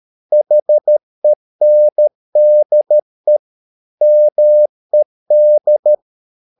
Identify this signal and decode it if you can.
Morse code, heard as a keyed tone, about 13 wpm, 600 Hz. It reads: HENDE MED
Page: Ferie 13 wpm